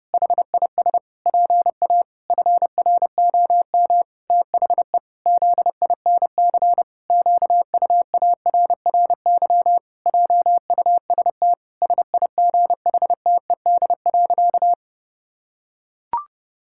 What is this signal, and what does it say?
Morse code, heard as a keyed tone, about 32 wpm, 700 Hz. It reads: HIS PA FROM THE ZINC QUARRY JUST SIGHTED.
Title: Pangram Challenge v2 - 30wpm